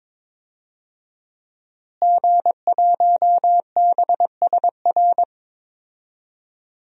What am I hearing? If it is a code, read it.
Morse code, heard as a keyed tone, about 22 wpm, 700 Hz. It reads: G1BSR
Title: DXCC-100 Call Signs - 5 Characters - Repeated 3x - 22wpm